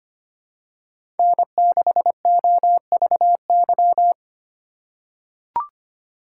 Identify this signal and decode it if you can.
Morse code, heard as a keyed tone, about 25 wpm, 700 Hz. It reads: N6OVY
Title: USA Call Signs - 5 Characters - 4x Character Spacing - 25wpm